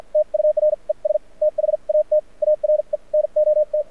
Morse code, heard as a keyed tone, about 56 wpm, spaced as for 44 wpm, 600 Hz. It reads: TURES THAT ARE NOT
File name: archived / CRSnet